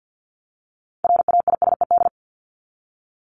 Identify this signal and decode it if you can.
Morse code, heard as a keyed tone, about 50 wpm, 700 Hz. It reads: RAISED
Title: ED Suffix - Top 100 Words with Base Words 50wpm